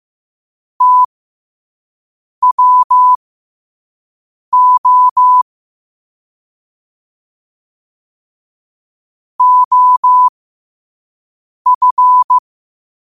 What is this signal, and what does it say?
Morse code, heard as a keyed tone, about 15 wpm, spaced as for 5 wpm, 1000 Hz. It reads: TWO OF
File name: qotdfiles / 5wpm